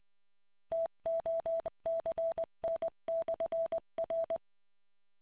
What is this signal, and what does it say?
Morse code, heard as a keyed tone, about 20 wpm, 660 Hz. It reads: TOCN/R